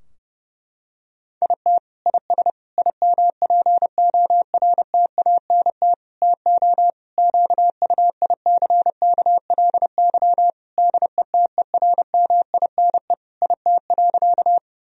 Morse code, heard as a keyed tone, about 32 wpm, 700 Hz. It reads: IT IS IMPORTANT TO QUICKLY DETERMINE IT.
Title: Sentences from Top 1000 Words - Review Part 4 - 30wpm